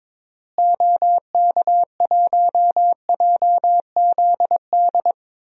Morse code, heard as a keyed tone, about 22 wpm, 700 Hz. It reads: OK1JZD
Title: DXCC-100 Call Signs - 6 Characters - Repeated 2x - 22wpm